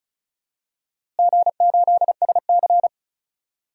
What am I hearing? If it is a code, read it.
Morse code, heard as a keyed tone, about 36 wpm, 700 Hz. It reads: G8SC
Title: DXCC-100 Call Signs - 4 Characters - Repeated 3x - 35wpm